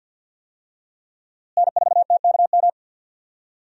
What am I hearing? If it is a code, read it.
Morse code, heard as a keyed tone, about 53 wpm, 700 Hz. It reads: N4TKM
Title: USA Call Signs - 5 Characters - Repeated 3x - 50wpm